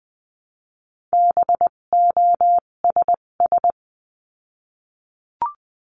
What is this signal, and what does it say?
Morse code, heard as a keyed tone, about 20 wpm, spaced as for 17 wpm, 700 Hz. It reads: BOSS
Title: 257 Lesson 42b - Words 14-20wpm